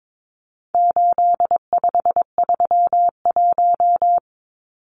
Morse code, heard as a keyed tone, about 22 wpm, 700 Hz. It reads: 8531
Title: Numbers - 4-Digit 22wpm